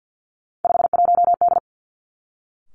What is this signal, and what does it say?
Morse code, heard as a keyed tone, about 50 wpm, 700 Hz. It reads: 51D
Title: Three Challenging Letter-Number Combinations 50wpm